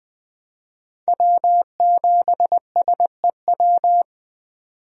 Morse code, heard as a keyed tone, about 20 wpm, 700 Hz. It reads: W7SEW
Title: USA Call Signs - 5 Characters - 20wpm